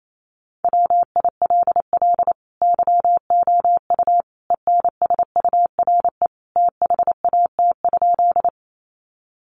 Morse code, heard as a keyed tone, about 28 wpm, 700 Hz. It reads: WILL YOU ENSURE THAT?
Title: Sentences from Top 1100 Words 28wpm